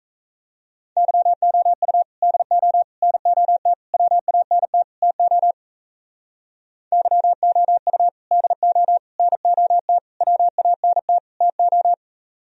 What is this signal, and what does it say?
Morse code, heard as a keyed tone, about 33 wpm, 700 Hz. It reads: YOU DO NOT WANT TO YOU DO NOT WANT TO
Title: Speed-Racing - Sets of 5 Words 28wpm